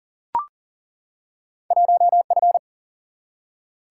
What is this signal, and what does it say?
Morse code, heard as a keyed tone, about 40 wpm, 700 Hz. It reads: 1F
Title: Single-Letter Single-Number 40wpm